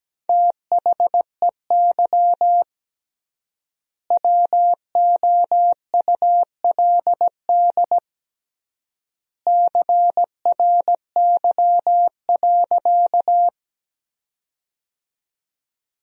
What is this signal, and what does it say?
Morse code, heard as a keyed tone, about 17 wpm, 700 Hz. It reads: THEY WOULD CRY.
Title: 3x Word-Spacing - Sentences from Top 500 Words - 17wpm